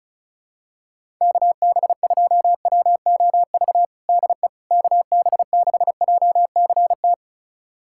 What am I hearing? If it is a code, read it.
Morse code, heard as a keyed tone, about 36 wpm, 700 Hz. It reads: KB2WOV DE KB6JCT
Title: USA Callsign DE Callsign - 6 Characters - 35wpm